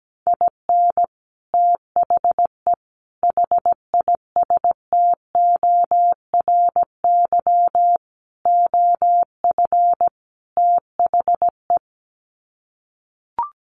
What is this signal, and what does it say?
Morse code, heard as a keyed tone, about 17 wpm, 700 Hz. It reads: IN THE HISTORY OF THE
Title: Sets of 5 Words 17wpm